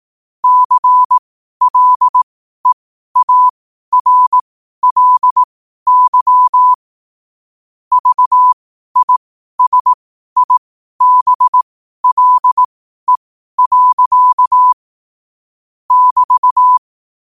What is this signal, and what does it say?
Morse code, heard as a keyed tone, about 18 wpm, spaced as for 12 wpm, 1000 Hz. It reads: CLEARLY VISIBLE. <BT>
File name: qotdfiles / 12wpm